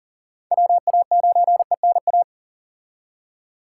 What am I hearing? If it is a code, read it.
Morse code, heard as a keyed tone, about 37 wpm, 700 Hz. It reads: WA9ENA
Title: K1USN SST QSO Practice - 40wpm